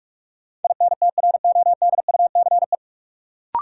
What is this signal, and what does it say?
Morse code, heard as a keyed tone, about 41 wpm, 700 Hz. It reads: INTRODUCE
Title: Sentences from Top 1600 Words - New Word Drill Alt - 45wpm